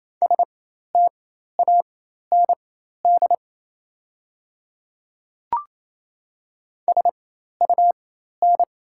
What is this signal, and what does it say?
Morse code, heard as a keyed tone, about 27 wpm, spaced as for 13 wpm, 690 Hz. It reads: STAND SUN
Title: 4x Character Spacing - Top 300 Words - 28wpm